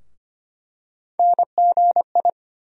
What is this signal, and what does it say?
Morse code, heard as a keyed tone, about 25 wpm, 700 Hz. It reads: NGI
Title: USA Call Sign Postfixes - 3 Letters 25wpm